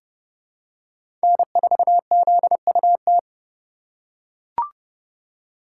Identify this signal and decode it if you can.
Morse code, heard as a keyed tone, about 32 wpm, 700 Hz. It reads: N4ZUT
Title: USA Call Signs - 5 Characters - 4x Character Spacing - 30wpm